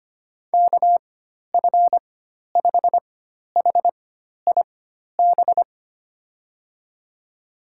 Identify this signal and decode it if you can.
Morse code, heard as a keyed tone, about 25 wpm, spaced as for 12 wpm, 700 Hz. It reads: KF5HIB
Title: USA Call Signs - 6 Characters - 4x Character Spacing - 25wpm